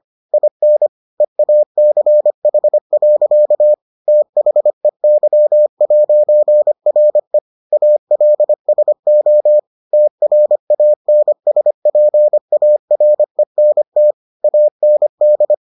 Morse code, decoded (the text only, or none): IN EACH. THEY'RE ALSO TRANSPARENT AND